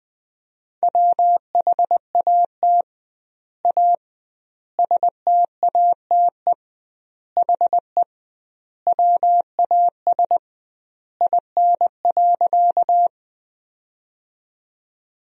Morse code, heard as a keyed tone, about 20 wpm, 700 Hz. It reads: WHAT A STATE HE WAS IN.
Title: 2x Word-Spacing - Sentences from Top 400 Words - 20wpm